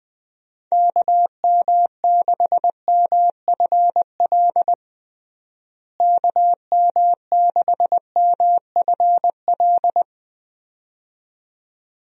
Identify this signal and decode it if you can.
Morse code, heard as a keyed tone, about 21 wpm, 700 Hz. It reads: KM6MFL KM6MFL
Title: USA Call Signs - 6 Characters - Repeated 2x - 20wpm